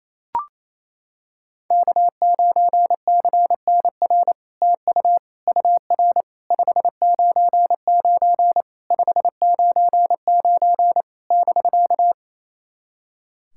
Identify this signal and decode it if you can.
Morse code, heard as a keyed tone, about 27 wpm, 700 Hz. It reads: K9CNR TU UR 599 599 <BK>
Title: SOTA QSO Practice 28wpm